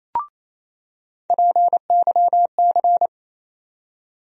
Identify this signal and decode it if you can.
Morse code, heard as a keyed tone, about 28 wpm, 700 Hz. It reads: PYC
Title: USA Call Sign Postfixes - 3 Letters 28wpm